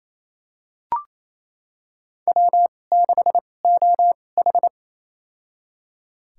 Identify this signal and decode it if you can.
Morse code, heard as a keyed tone, about 28 wpm, spaced as for 20 wpm, 700 Hz. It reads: W6OH
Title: USA Call Signs - 2x Character Spacing - 28wpm